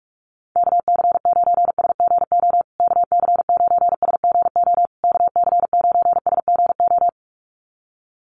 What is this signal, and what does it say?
Morse code, heard as a keyed tone, about 44 wpm, 700 Hz. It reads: KC9SGO KC9SGO KC9SGO
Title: USA Call Signs Repeated 3x 45wpm